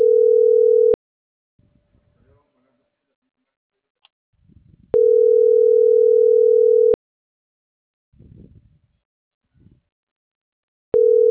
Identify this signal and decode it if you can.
Ringback tone (North America), 440 + 480 Hz.